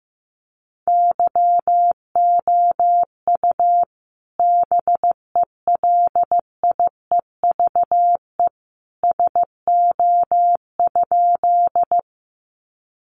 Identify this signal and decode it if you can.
Morse code, heard as a keyed tone, about 15 wpm, 700 Hz. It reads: YOU BELIEVE SO?
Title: Sentences from Top 1000 Words 15wpm